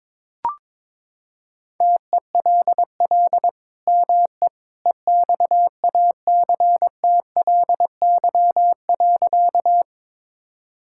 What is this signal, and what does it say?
Morse code, heard as a keyed tone, about 21 wpm, 700 Hz. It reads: TELL ME EXACTLY.
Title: Sentences from Top 1000 Words - Review Part 6 - 22wpm